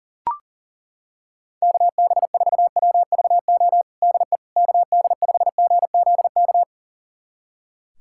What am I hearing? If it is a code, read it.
Morse code, heard as a keyed tone, about 40 wpm, 700 Hz. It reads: KB4WVO DE KD5GZK
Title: USA Callsign DE Callsign 40wpm